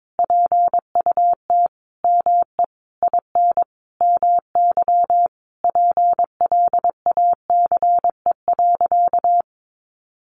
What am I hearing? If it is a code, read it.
Morse code, heard as a keyed tone, about 22 wpm, 700 Hz. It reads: PUT ME IN MY PLACE.
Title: Sentences from Top 1200 Words 22wpm